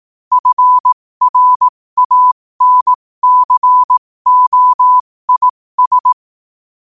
Morse code, heard as a keyed tone, about 18 wpm, spaced as for 15 wpm, 1000 Hz. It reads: FRANCOIS
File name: qotdfiles / 15wpm